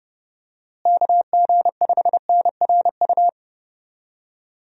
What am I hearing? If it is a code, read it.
Morse code, heard as a keyed tone, about 30 wpm, 700 Hz. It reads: KG5NRU
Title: USA Call Signs - 6 Characters - Repeated 3x - 30wpm